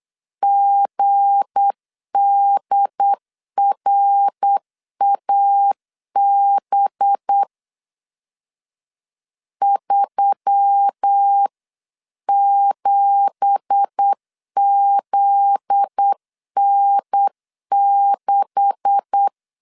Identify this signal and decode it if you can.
Morse code, heard as a keyed tone, about 8 wpm, 800 Hz. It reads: GDRAB 37ZN6
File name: 8wpm